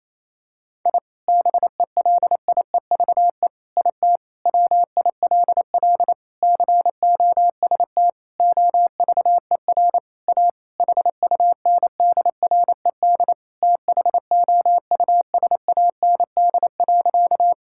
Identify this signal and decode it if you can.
Morse code, heard as a keyed tone, about 28 wpm, 700 Hz. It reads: I BELIEVE IT WILL COST OVER A HUNDRED THOUSAND.
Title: Sentences from Top 1000 Words - Review Part 5 - 28wpm